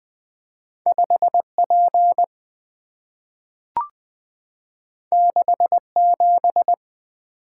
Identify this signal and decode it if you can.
Morse code, heard as a keyed tone, about 20 wpm, 690 Hz. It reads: 5P 67
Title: Two Letter-Number Combinations - 20wpm